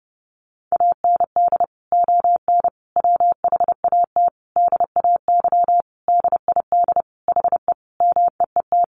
Morse code, heard as a keyed tone, about 30 wpm, 700 Hz. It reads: AND ON WHAT DAY DID HE MEET
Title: Sentences from Top 1100 Words 30wpm